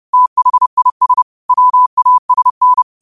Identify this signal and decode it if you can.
Morse code, heard as a keyed tone, about 30 wpm, 1000 Hz. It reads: THIS WASN
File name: qotdfiles / 30wpm